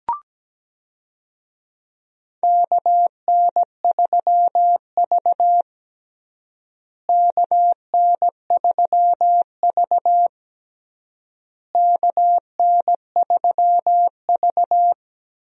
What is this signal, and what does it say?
Morse code, heard as a keyed tone, about 17 wpm, 700 Hz. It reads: KN3V KN3V KN3V
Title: USA Call Signs - 4 Characters - Repeated 3x - 17wpm